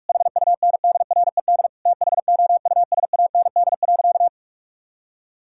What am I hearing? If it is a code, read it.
Morse code, heard as a keyed tone, about 41 wpm, 700 Hz. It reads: HUNDRED THOUSAND.
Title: Sentences from Top 1000 Words - Review Part 5 - 45wpm